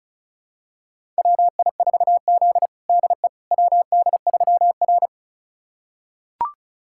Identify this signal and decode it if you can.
Morse code, heard as a keyed tone, about 36 wpm, 700 Hz. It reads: WI4Z DE WD3R
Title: USA Callsign DE Callsign - 4 Characters - 35wpm